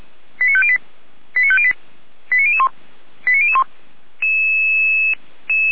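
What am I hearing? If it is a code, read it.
Five-tone selective call (ZVEI), heard as quick runs of five tones, tones of about 70 ms. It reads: 89508 89508 89001 89001
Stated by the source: ZVEI five-tone call received by SDR.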